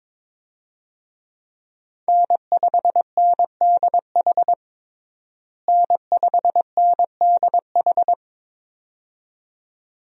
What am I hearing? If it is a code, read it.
Morse code, heard as a keyed tone, about 22 wpm, 700 Hz. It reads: N5NDH N5NDH
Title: USA Call Signs - 5 Characters - Repeated 2x - 22wpm